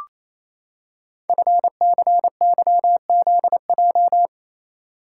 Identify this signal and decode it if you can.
Morse code, heard as a keyed tone, about 29 wpm, 700 Hz. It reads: FCYZJ
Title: Code Groups - 5 Letters - 28wpm